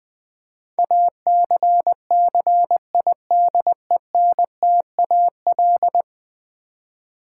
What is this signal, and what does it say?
Morse code, heard as a keyed tone, about 20 wpm, 700 Hz. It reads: ACCIDENTAL
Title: Build A Word - 5 Letters At A Time - 12-Letter Words - 20wpm